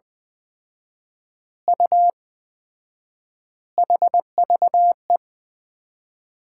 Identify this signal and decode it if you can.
Morse code, heard as a keyed tone, about 20 wpm, 700 Hz. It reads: U HVE
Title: Common QSO Phrases - 4x Word-Spacing - 20wpm